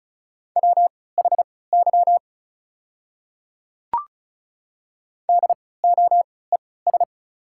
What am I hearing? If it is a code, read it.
Morse code, heard as a keyed tone, about 36 wpm, spaced as for 20 wpm, 690 Hz. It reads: WHY DOES
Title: 3x Character Spacing - Top 200 Words - 35wpm